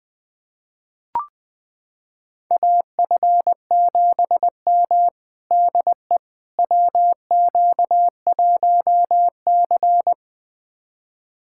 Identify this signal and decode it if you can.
Morse code, heard as a keyed tone, about 20 wpm, 700 Hz. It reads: AF7M DE WQ1C
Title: USA Callsign DE Callsign - 4 Characters - 20wpm